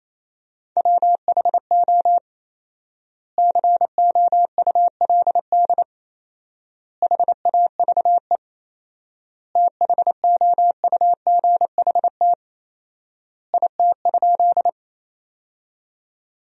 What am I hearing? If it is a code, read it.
Morse code, heard as a keyed tone, about 28 wpm, 700 Hz. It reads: WHO COULD HAVE THOUGHT IT?
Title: 4x Word-Spacing - Sentences from Top 300 Words - 28wpm